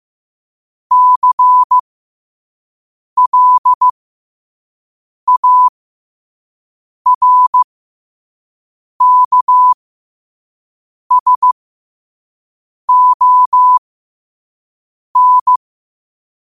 Morse code, heard as a keyed tone, about 15 wpm, spaced as for 5 wpm, 1000 Hz. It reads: CLARKSON
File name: qotdfiles / 5wpm